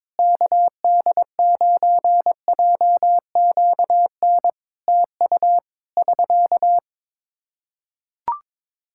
Morse code, heard as a keyed tone, about 22 wpm, 700 Hz. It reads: KD9JQN TU <SK>
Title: K1USN SST QSO Practice 22wpm